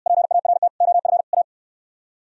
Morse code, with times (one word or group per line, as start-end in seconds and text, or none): 0.1-0.7 PART
0.8-1.2 OF
1.3-1.4 A